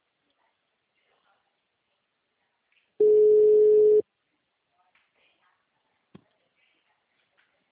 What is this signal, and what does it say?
Ringback tone (Europe, 425 Hz), with speech.